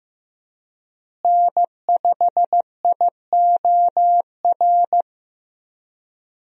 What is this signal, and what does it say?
Morse code, heard as a keyed tone, about 15 wpm, 700 Hz. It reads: N5IOR
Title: USA Call Signs - 5 Characters - Repeated 3x - 15wpm